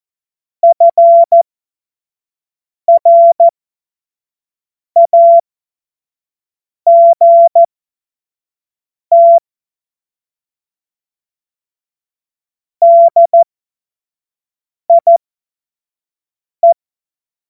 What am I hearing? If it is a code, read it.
Morse code, heard as a keyed tone, about 14 wpm, spaced as for 5 wpm, 680 Hz. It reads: FRAGT DIE
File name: Morsekurs / Witze / Witz 26. Oktober 2021, 25 F70